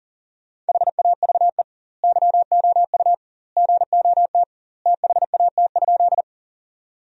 Morse code, heard as a keyed tone, about 40 wpm, 700 Hz. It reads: HAVE YOU GOT THAT?
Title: 2x Word-Spacing - Sentences from Top 300 Words - 40wpm